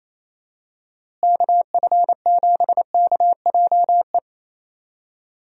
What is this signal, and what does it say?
Morse code, heard as a keyed tone, about 28 wpm, 700 Hz. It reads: KF7KJE
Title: USA Call Signs - 6 Characters - Repeated 2x - 28wpm